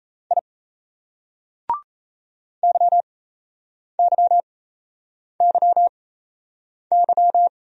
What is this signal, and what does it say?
Morse code, heard as a keyed tone, about 37 wpm, spaced as for 8 wpm, 700 Hz. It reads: I YYYY